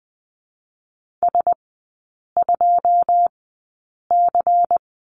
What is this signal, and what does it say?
Morse code, heard as a keyed tone, about 20 wpm, spaced as for 8 wpm, 700 Hz. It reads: S2C
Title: 107 Lesson 18e - Letter m - Three Characters 20wpm